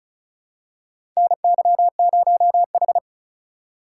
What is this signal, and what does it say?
Morse code, heard as a keyed tone, about 36 wpm, 700 Hz. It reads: NY0H
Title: USA Call Signs - 4 Characters - 3x Character Spacing - 35wpm